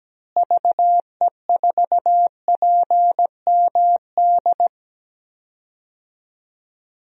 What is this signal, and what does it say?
Morse code, heard as a keyed tone, about 17 wpm, 700 Hz. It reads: VE4PMD
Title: DXCC-100 Call Signs - 6 Characters - Repeated 3x - 17wpm